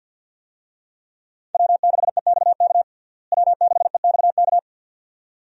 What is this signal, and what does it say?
Morse code, heard as a keyed tone, about 51 wpm, 700 Hz. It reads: W6EXK W6EXK
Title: USA Call Signs - 5 Characters - Repeated 2x - 50wpm